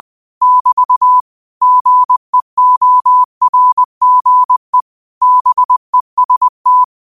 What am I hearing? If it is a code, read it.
Morse code, heard as a keyed tone, about 20 wpm, 1000 Hz. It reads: <BT> GEORGE BEST
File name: 20wpm